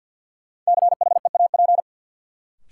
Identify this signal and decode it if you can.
Morse code, heard as a keyed tone, about 53 wpm, 700 Hz. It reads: CHEAP
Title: Sentences from Top 1200 Words - New Word Drill Alt - 50wpm